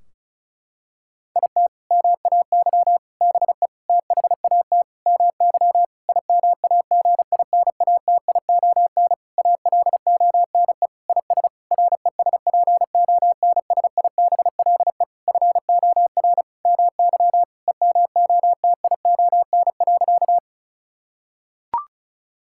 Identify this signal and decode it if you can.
Morse code, heard as a keyed tone, about 36 wpm, 700 Hz. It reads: IT MAY BE THAT MY IMAGINATION ALONE IS RESPONSIBLE FOR MY EMOTION.
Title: Sentences from Top 1600 Words 35wpm